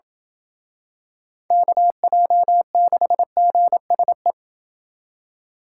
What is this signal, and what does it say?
Morse code, heard as a keyed tone, about 28 wpm, 700 Hz. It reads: KJ6GSE